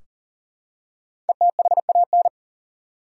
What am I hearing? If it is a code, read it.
Morse code, heard as a keyed tone, about 40 wpm, 700 Hz. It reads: ETHAN
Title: USA Common Names - 40wpm